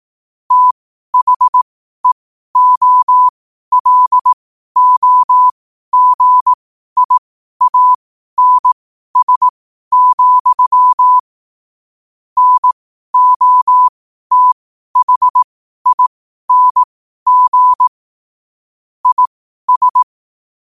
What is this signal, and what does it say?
Morse code, heard as a keyed tone, about 18 wpm, spaced as for 12 wpm, 1000 Hz. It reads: THEOLOGIANS, NOTHING IS
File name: qotdfiles / 12wpm